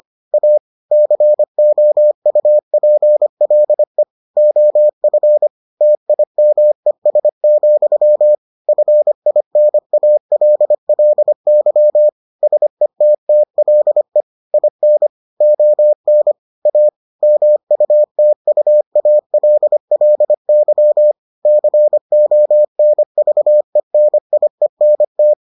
Morse code, heard as a keyed tone, about 25 wpm, 600 Hz. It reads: A COUPLE OF TIMES, FINALLY SETTLE IN ON A MUTUALLY CONVENIENT